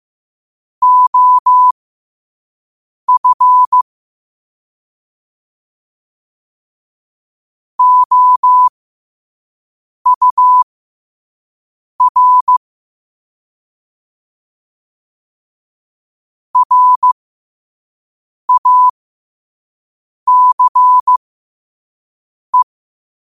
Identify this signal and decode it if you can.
Morse code, heard as a keyed tone, about 15 wpm, spaced as for 5 wpm, 1000 Hz. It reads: OF OUR RACE
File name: qotdfiles / 5wpm